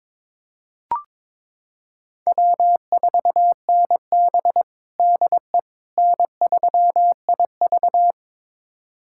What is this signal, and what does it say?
Morse code, heard as a keyed tone, about 22 wpm, 700 Hz. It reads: W4NB DE N3IV
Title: USA Callsign DE Callsign - 4 Characters - 22wpm